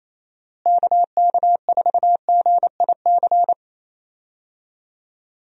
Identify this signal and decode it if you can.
Morse code, heard as a keyed tone, about 28 wpm, 700 Hz. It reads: KK4GIC
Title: USA Call Signs - 6 Characters - Repeated 2x - 28wpm